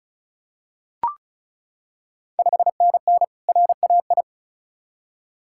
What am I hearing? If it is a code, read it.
Morse code, heard as a keyed tone, about 36 wpm, 690 Hz. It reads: 5NN RAI
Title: Ontario QSO Party Exchange 35wpm